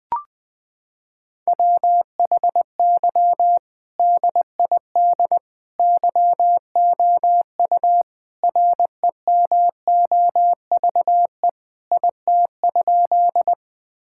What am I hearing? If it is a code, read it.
Morse code, heard as a keyed tone, about 20 wpm, 700 Hz. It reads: WHY DID YOU REMOVE IT?
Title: Sentences from Top 1000 Words - Review Part 5 - 20wpm